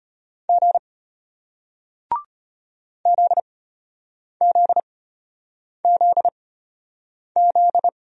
Morse code, heard as a keyed tone, about 34 wpm, spaced as for 8 wpm, 700 Hz. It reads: G ZZZZ